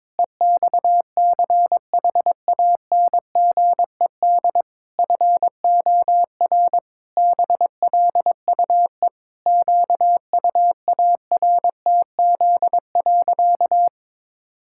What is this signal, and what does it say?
Morse code, heard as a keyed tone, about 22 wpm, 700 Hz. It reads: EXCHANGED FOR BLUE QUARTZ.
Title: Pangram Challenge v2 - 22wpm